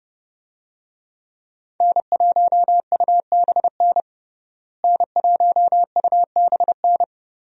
Morse code, heard as a keyed tone, about 30 wpm, 700 Hz. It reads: N1UBN N1UBN
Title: USA Call Signs - 5 Characters - Repeated 2x - 30wpm